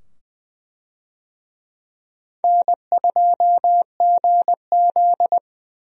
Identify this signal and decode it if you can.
Morse code, heard as a keyed tone, about 20 wpm, 700 Hz. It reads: N2GZ
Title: USA Call Signs - 4 Characters - 20wpm